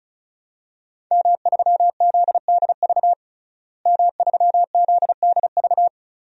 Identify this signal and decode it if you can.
Morse code, heard as a keyed tone, about 36 wpm, 700 Hz. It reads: M3ZDV M3ZDV
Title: DXCC-100 Call Signs - 5 Characters - Repeated 3x - 35wpm